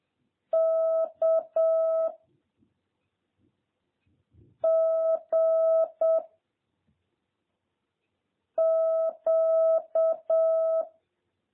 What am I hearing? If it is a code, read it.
Morse code, heard as a keyed tone, about 7 wpm, 660 Hz. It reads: K G Q